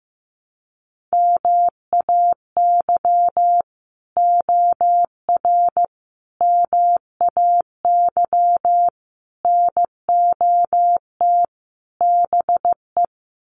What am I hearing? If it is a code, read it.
Morse code, heard as a keyed tone, about 15 wpm, 700 Hz. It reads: MAY OR MAY NOT BE
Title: Sets of 5 Words 15wpm